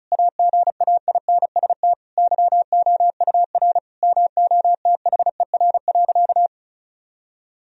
Morse code, heard as a keyed tone, about 36 wpm, 700 Hz. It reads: AGAINST YOUR MOTHER.
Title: Sentences from Top 1000 Words - Review Part 2 - 35wpm